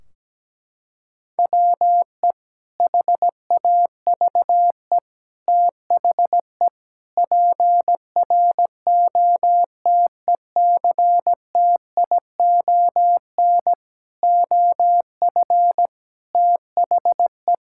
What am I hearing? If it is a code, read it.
Morse code, heard as a keyed tone, about 17 wpm, 700 Hz. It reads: WE HAVE THE PROTECTION OF THE
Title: Sentences from Top 1600 Words 17wpm